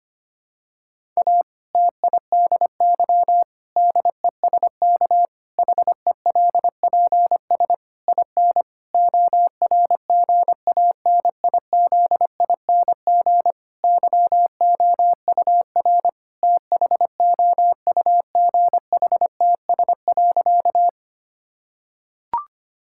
Morse code, heard as a keyed tone, about 25 wpm, 700 Hz. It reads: A TIDY DESK HELPS IN ORGANIZING YOUR THOUGHTS.
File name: practice / 25wpm / Difficult Sentences with Hints - 25wpm